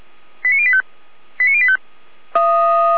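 Five-tone selective call (ZVEI), heard as quick runs of five tones, tones of about 70 ms. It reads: 89085 89085
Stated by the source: ZVEI five-tone selective call received on an SDR.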